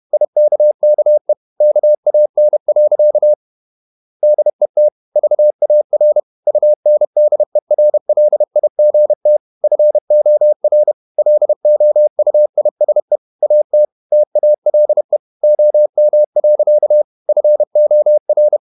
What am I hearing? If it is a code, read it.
Morse code, heard as a keyed tone, about 31 wpm, 600 Hz. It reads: IKKE KAN. DET VAR UNDERLIGT FOR LOUISE AT TALE OM. FOR
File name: Ferie QRQ 31wpm